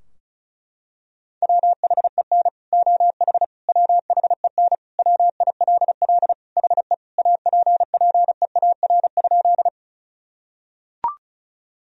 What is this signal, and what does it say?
Morse code, heard as a keyed tone, about 36 wpm, 700 Hz. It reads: WHEN OH WHEN WILL HE APPEAR?
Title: Sentences from Top 1000 Words - Review Part 2 - 35wpm